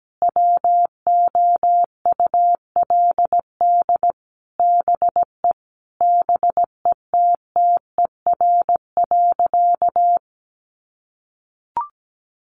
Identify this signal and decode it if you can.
Morse code, heard as a keyed tone, about 17 wpm, 700 Hz. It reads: WOULD BE BETTER.
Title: Sentences from Top 400 Words 17wpm